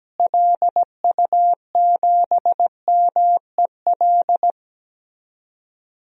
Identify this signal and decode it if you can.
Morse code, heard as a keyed tone, about 17 wpm, 700 Hz. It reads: LU7MEL
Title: DXCC-100 Call Signs - 6 Characters - Repeated 3x - 17wpm